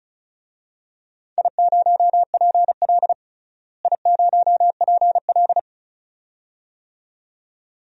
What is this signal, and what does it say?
Morse code, heard as a keyed tone, about 36 wpm, 700 Hz. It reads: I0PL I0PL
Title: DXCC-100 Call Signs - 4 Characters - Repeated 2x - 35wpm